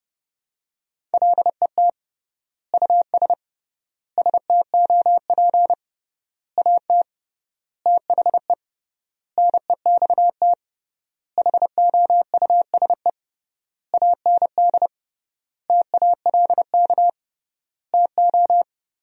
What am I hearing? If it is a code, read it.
Morse code, heard as a keyed tone, about 30 wpm, 700 Hz. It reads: LET US STOP AT THE NEXT HOUSE AND TALK TO
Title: 3x Word-Spacing - Sentences from Top 400 Words - 30wpm